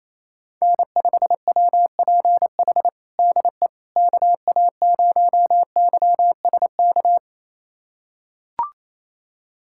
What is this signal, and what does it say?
Morse code, heard as a keyed tone, about 28 wpm, 700 Hz. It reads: N5WPH DE KA0YSK
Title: USA Callsign DE Callsign 28wpm